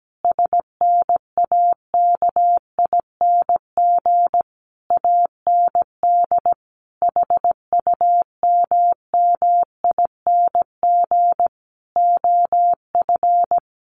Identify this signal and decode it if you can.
Morse code, heard as a keyed tone, about 17 wpm, 700 Hz. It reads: SNAKING AND HUMMING OF
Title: A Sound Of Thunder 17wpm